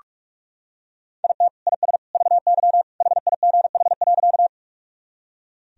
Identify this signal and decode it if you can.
Morse code, heard as a keyed tone, about 42 wpm, 700 Hz. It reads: IT IS VY HIGH.
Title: Sentences from Top 200 Words and Abbreviations 45wpm